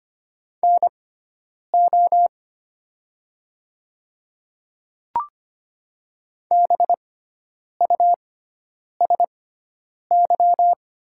Morse code, heard as a keyed tone, about 25 wpm, spaced as for 9 wpm, 700 Hz. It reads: NO BUSY
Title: 6x Character Spacing - Top 500 Words - 25wpm